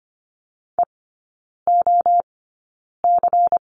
Morse code, heard as a keyed tone, about 25 wpm, spaced as for 9 wpm, 700 Hz. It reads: EOC
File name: course / 25:10wpm / 093 Lesson 16e - Letter c - Three Characters 25wpm